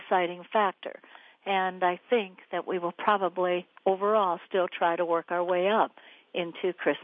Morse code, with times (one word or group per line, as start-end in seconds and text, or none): none